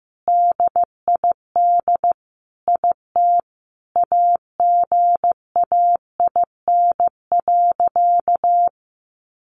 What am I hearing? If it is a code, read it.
Morse code, heard as a keyed tone, about 15 wpm, 700 Hz. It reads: DID IT AGAIN.
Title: Sentences from Top 300 Words 15wpm